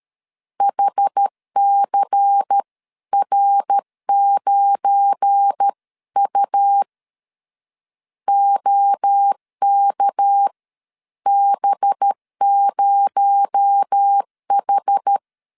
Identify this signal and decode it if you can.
Morse code, heard as a keyed tone, about 13 wpm, spaced as for 11 wpm, 800 Hz. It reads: HCR9U OKB0H
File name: LEZIONI / 7 GF9Z / 7 GF9 12wpm 2319